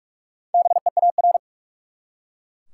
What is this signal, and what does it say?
Morse code, heard as a keyed tone, about 42 wpm, 700 Hz. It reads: BEAR
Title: Four Letter Words - 45wpm